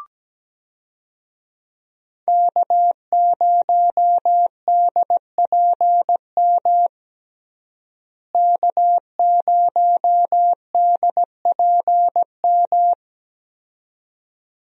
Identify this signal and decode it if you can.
Morse code, heard as a keyed tone, about 17 wpm, 700 Hz. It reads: K0DPM K0DPM
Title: USA Call Signs - 5 Characters - Repeated 2x - 17wpm